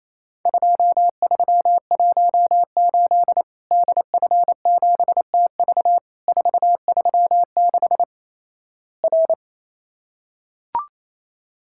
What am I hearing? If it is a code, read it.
Morse code, heard as a keyed tone, about 28 wpm, 700 Hz. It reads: 2318 DF7TV 436 R
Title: Worked All Europe DX Contest - QTC Exchange - 28wpm